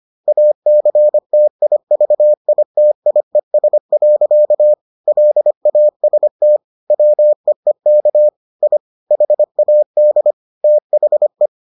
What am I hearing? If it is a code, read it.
Morse code, heard as a keyed tone, about 25 wpm, 600 Hz. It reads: ACTIVITIES. LAST WEEK I HAD THE